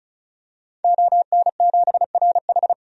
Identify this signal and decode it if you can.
Morse code, heard as a keyed tone, about 35 wpm, 700 Hz. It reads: ON7RH
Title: DXCC-100 Call Signs - 5 Characters - 35wpm